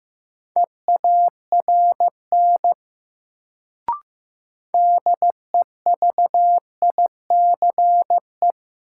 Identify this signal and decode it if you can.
Morse code, heard as a keyed tone, about 15 wpm, 700 Hz. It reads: EARN DEVICE
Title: Sentences from Top 1500 Words - New Word Drill Alt - 15wpm